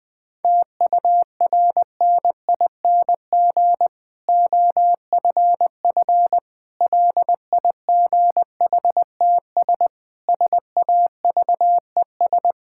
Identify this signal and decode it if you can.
Morse code, heard as a keyed tone, about 20 wpm, 700 Hz. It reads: TURNING OFF LIGHTS SAVES
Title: Difficult Sentences with Hints - 20wpm